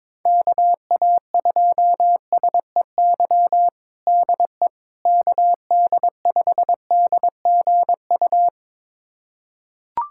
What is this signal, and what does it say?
Morse code, heard as a keyed tone, about 22 wpm, 700 Hz. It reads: KA2SEY DE KD5DGU
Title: USA Callsign DE Callsign 22wpm